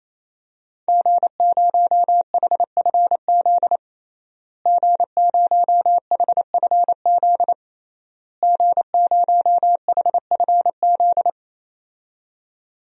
Morse code, heard as a keyed tone, about 28 wpm, 700 Hz. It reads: G0HFZ G0HFZ G0HFZ
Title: DXCC-100 Call Signs - 5 Characters - Repeated 3x - 28wpm